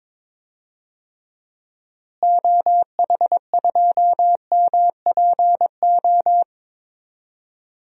Morse code, heard as a keyed tone, about 22 wpm, 700 Hz. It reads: OH2MPO
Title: DXCC-100 Call Signs - 6 Characters - Repeated 3x - 22wpm